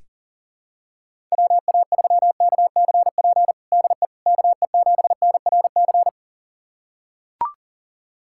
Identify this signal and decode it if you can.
Morse code, heard as a keyed tone, about 40 wpm, 700 Hz. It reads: WA3KCP DE KE7NRC
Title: USA Callsign DE Callsign - 6 Characters - 40wpm